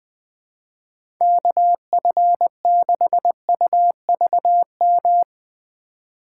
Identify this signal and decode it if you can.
Morse code, heard as a keyed tone, about 20 wpm, 700 Hz. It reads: KF6UVM
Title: USA Call Signs - 6 Characters - 4x Character Spacing - 20wpm